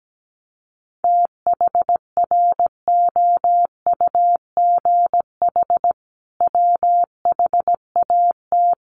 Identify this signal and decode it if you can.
Morse code, heard as a keyed tone, about 17 wpm, 700 Hz. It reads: THROUGH WHAT
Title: Sentences from Top 900 Words 17wpm